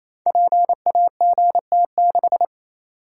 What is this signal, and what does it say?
Morse code, heard as a keyed tone, about 27 wpm, 700 Hz. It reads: PAGT6
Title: Code Groups - 5 Letter:Numbers - 28wpm